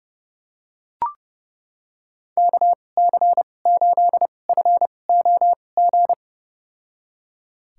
Morse code, heard as a keyed tone, about 30 wpm, spaced as for 22 wpm, 700 Hz. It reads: KC8FOG
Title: USA Call Signs - 2x Character Spacing - 30wpm